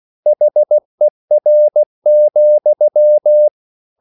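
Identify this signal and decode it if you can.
Morse code, heard as a keyed tone, about 16 wpm, 600 Hz. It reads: HER,
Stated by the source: Slow (QRS) Morse code at 16 wpm, from CW med Gnister.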